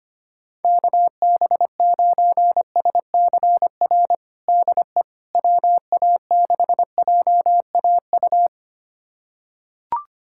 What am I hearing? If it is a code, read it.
Morse code, heard as a keyed tone, about 25 wpm, 700 Hz. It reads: KB9SCR DE WA6JAU
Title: USA Callsign DE Callsign - 6 Characters - 25wpm